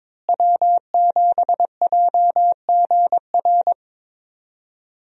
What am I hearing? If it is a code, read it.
Morse code, heard as a keyed tone, about 22 wpm, 700 Hz. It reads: W7JGR
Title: USA Call Signs - 5 Characters - Repeated 3x - 22wpm